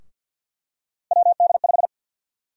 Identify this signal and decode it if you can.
Morse code, heard as a keyed tone, about 53 wpm, 700 Hz. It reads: WD5
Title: USA Call Sign Prefixes - Repeated 2x - 50wpm